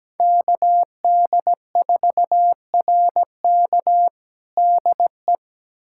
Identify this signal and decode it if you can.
Morse code, heard as a keyed tone, about 17 wpm, 700 Hz. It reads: KD4RK DE
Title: USA Callsign DE Callsign 17wpm